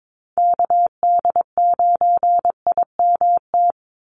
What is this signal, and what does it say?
Morse code, heard as a keyed tone, about 22 wpm, 700 Hz. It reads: KD9IMT
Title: USA Call Signs 22wpm